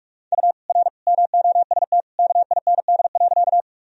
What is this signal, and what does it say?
Morse code, heard as a keyed tone, about 46 wpm, 700 Hz. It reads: U R MOST KIND.
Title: Sentences from Top 200 Words and Abbreviations 45wpm